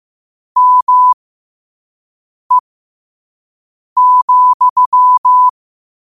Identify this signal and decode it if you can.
Morse code, heard as a keyed tone, about 15 wpm, spaced as for 5 wpm, 1000 Hz. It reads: ME,
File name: qotdfiles / 5wpm